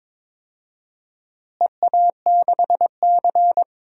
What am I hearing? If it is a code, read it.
Morse code, heard as a keyed tone, about 22 wpm, 700 Hz. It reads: EA6C
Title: DXCC-100 Call Signs - 4 Characters - Repeated 2x - 22wpm